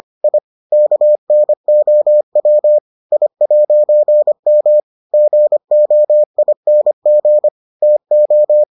Morse code, heard as a keyed tone, about 25 wpm, 600 Hz. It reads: I KNOW I'M GOING TO